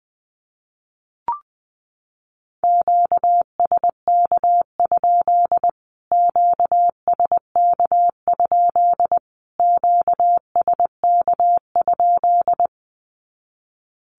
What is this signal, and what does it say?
Morse code, heard as a keyed tone, about 20 wpm, 700 Hz. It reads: QSK? QSK? QSK?
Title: Q-Codes Repeated 3x 20wpm